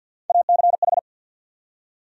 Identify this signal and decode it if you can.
Morse code, heard as a keyed tone, about 48 wpm, 700 Hz. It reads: ACH
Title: USA Call Sign Postfixes - 3 Letters 50wpm